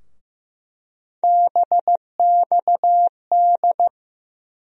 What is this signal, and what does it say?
Morse code, heard as a keyed tone, about 15 wpm, 700 Hz. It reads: BXD
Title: USA Call Sign Postfixes - 3 Letters 15wpm